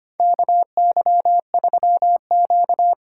Morse code, heard as a keyed tone, about 25 wpm, 700 Hz. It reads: KY3Q
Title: USA Call Signs - 2x Character Spacing - 25wpm